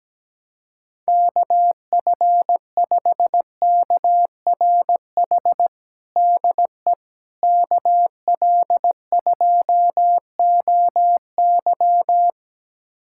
Morse code, heard as a keyed tone, about 17 wpm, 700 Hz. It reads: KF5KRH DE KL2OY
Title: USA Callsign DE Callsign 17wpm